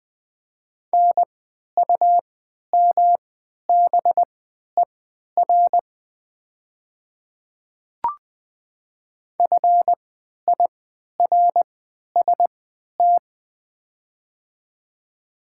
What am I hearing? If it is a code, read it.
Morse code, heard as a keyed tone, about 20 wpm, spaced as for 11 wpm, 700 Hz. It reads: NUMBER FIRST
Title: 3x Character Spacing - Top 200 Words - 20wpm